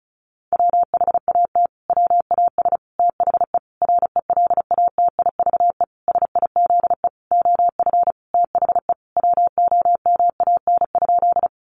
Morse code, heard as a keyed tone, about 34 wpm, 700 Hz. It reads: WHAT WAS THE RELATIVE SIZE OF THE WOMAN?
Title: Sentences from Top 1400 Words 35wpm